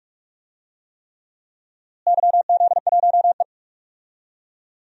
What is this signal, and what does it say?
Morse code, heard as a keyed tone, about 46 wpm, 700 Hz. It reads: YZ1E
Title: DXCC-100 Call Signs - 4 Characters - Repeated 3x - 45wpm